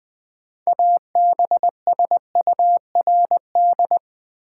Morse code, heard as a keyed tone, about 20 wpm, 700 Hz. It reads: ABSURD
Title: 6 Letter Words - First 3 Letters Given - 20wpm